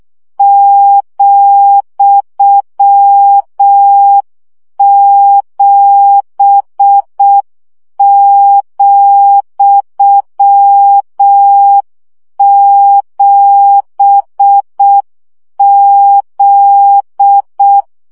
Morse code, heard as a keyed tone, about 6 wpm, 800 Hz. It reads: ,7,7Z
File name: CW gyakorlas z 7 vesszo 5 baud